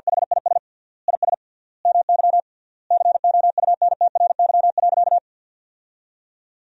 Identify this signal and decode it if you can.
Morse code, heard as a keyed tone, about 50 wpm, 700 Hz. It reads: HIS IS MY COUNTRY.
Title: 3x Word-Spacing - Sentences from Top 300 Words - 50wpm